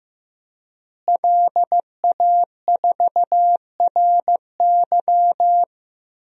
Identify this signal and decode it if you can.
Morse code, heard as a keyed tone, about 15 wpm, 700 Hz. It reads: LA4RY
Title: DXCC-100 Call Signs - 5 Characters - Repeated 3x - 15wpm